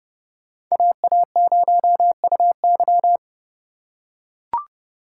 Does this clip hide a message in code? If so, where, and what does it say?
Yes: Morse code from 0.7 to 3.2 s, heard as a keyed tone, about 29 wpm, 700 Hz. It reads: AA0UY